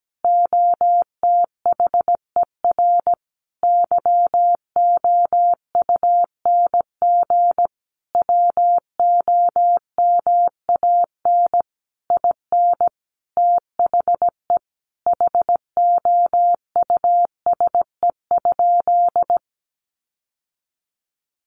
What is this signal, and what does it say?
Morse code, heard as a keyed tone, about 17 wpm, 700 Hz. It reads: OTHER YOUNG WOMAN IN THE HOUSE?
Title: Sentences from Top 1100 Words 17wpm